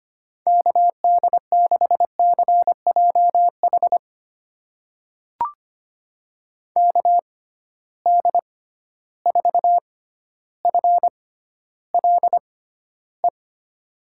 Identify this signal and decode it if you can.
Morse code, heard as a keyed tone, about 25 wpm, spaced as for 13 wpm, 700 Hz. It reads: KD6CJH KD4FLE
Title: USA Call Signs - 6 Characters - 6x Character Spacing - 25wpm